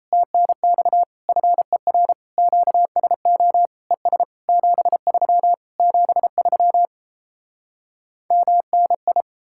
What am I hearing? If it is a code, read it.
Morse code, heard as a keyed tone, about 37 wpm, 700 Hz. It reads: TNX FER QSO ES 73 73 MNI